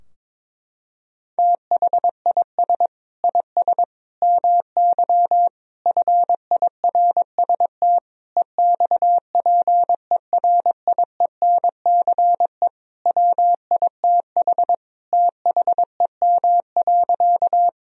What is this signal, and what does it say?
Morse code, heard as a keyed tone, about 22 wpm, 700 Hz. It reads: THIS IS MY FIRST EXPERIENCE WITH THEM.
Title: Sentences from Top 1000 Words - Review Part 3 - 22wpm